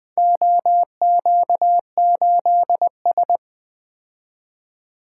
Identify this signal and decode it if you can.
Morse code, heard as a keyed tone, about 20 wpm, 700 Hz. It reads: OQ8S
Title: Four Letter-Number Combinations - 20wpm